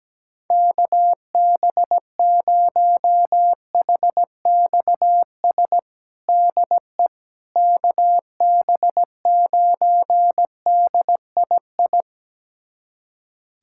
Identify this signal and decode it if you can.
Morse code, heard as a keyed tone, about 17 wpm, 700 Hz. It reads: KB0HXS DE KB9DII
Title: USA Callsign DE Callsign - 6 Characters - 17wpm